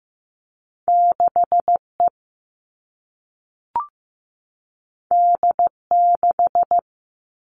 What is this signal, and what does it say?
Morse code, heard as a keyed tone, about 15 wpm, 700 Hz. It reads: 6E D6
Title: All Cut Numbers - 2-Digit 15wpm